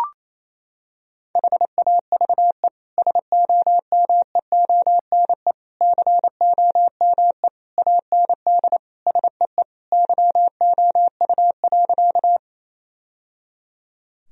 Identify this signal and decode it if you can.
Morse code, heard as a keyed tone, about 28 wpm, 700 Hz. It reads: HAVE SOMEONE COME AND SEE YOU.
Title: Sentences from Top 1000 Words - Review Part 4 - 28wpm